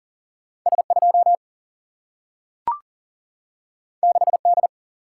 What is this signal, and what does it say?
Morse code, heard as a keyed tone, about 39 wpm, 700 Hz. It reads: S2 6D
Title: Single-Letter Single-Number 40wpm